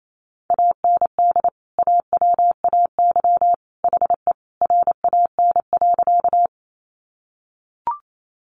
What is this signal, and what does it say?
Morse code, heard as a keyed tone, about 28 wpm, 700 Hz. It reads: AND AWAY HE RAN.
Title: Sentences from Top 600 Words 28wpm